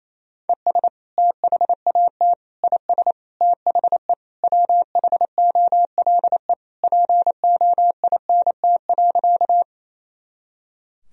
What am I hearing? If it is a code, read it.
Morse code, heard as a keyed tone, about 28 wpm, 700 Hz. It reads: ES THAT IS THE WHOLE POINT.
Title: Sentences from Top 400 Words and Abbreviations 28wpm